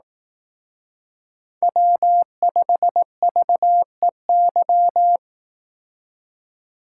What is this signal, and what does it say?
Morse code, heard as a keyed tone, about 18 wpm, 700 Hz. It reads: W5VEY
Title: Speed Racing - USA 1x3 Call Signs 15wpm